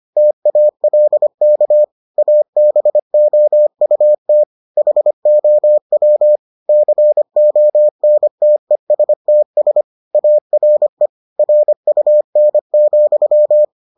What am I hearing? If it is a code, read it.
Morse code, heard as a keyed tone, about 25 wpm, 600 Hz. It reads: TALK ABOUT HOW CONTESTS ARE RUN,